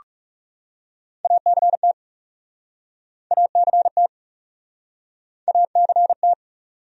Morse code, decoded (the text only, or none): ACT ACT ACT